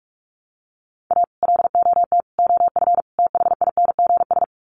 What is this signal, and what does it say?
Morse code, heard as a keyed tone, about 44 wpm, 700 Hz. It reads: A LOT OF THINGS
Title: Sets of 4 Words 45wpm